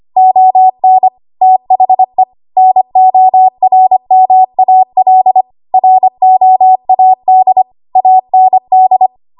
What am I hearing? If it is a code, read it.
Morse code, heard as a keyed tone, about 25 wpm, 750 Hz. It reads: ON THE NORMAL ROAD AND
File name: Archive / 25 WPM / 25WPM